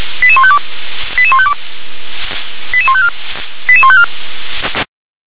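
Five-tone selective call (ZVEI), heard as quick runs of five tones, tones of about 70 ms. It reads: 88152 88152 88154 88154
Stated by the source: ZVEI five-tone selective call, code 88168, received by SDR.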